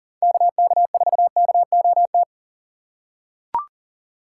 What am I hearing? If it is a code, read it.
Morse code, heard as a keyed tone, about 40 wpm, 700 Hz. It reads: KK4KOT
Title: USA Call Signs - 6 Characters - 3x Character Spacing - 40wpm